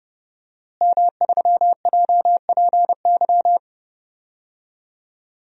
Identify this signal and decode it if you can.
Morse code, heard as a keyed tone, about 30 wpm, 700 Hz. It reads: M3JPY
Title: DXCC-100 Call Signs - 5 Characters - Repeated 3x - 30wpm